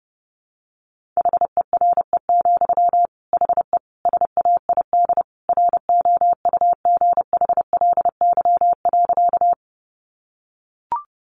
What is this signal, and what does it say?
Morse code, heard as a keyed tone, about 30 wpm, 700 Hz. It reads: HERE, HE SAID ROUGHLY.
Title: Sentences from Top 1500 Words 30wpm